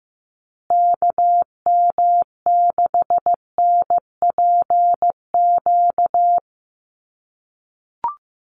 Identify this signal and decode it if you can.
Morse code, heard as a keyed tone, about 15 wpm, 700 Hz. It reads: KM6NPQ
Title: USA Call Signs Repeated 3x 15wpm